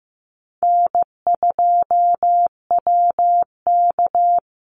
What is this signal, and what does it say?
Morse code, heard as a keyed tone, about 15 wpm, 700 Hz. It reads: N2WK
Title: Easy Call Signs Repeated 2x - 15wpm